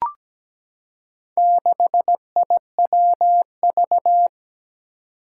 Morse code, heard as a keyed tone, about 17 wpm, 700 Hz. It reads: 6IWV
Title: Four Challenging Letter-Number Combinations 17wpm